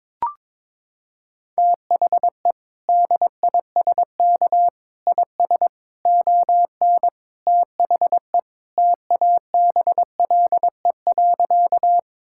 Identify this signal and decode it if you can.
Morse code, heard as a keyed tone, about 22 wpm, 700 Hz. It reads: THE DISK IS ON THE TABLE.
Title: Sentences from Top 1000 Words - Review Part 6 - 22wpm